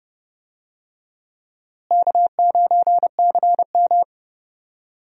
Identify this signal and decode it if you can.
Morse code, heard as a keyed tone, about 30 wpm, 700 Hz. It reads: K9CM
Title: USA Call Signs - 4 Characters - Repeated 2x - 30wpm